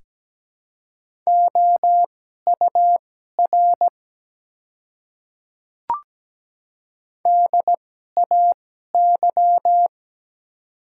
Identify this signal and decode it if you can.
Morse code, heard as a keyed tone, about 17 wpm, spaced as for 12 wpm, 700 Hz. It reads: OUR DAY
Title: 2x Character Spacing - Top 300 Words - 17wpm